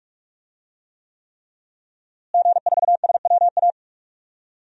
Morse code, heard as a keyed tone, about 46 wpm, 700 Hz. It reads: G4SWA
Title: DXCC-100 Call Signs - 5 Characters - Repeated 3x - 45wpm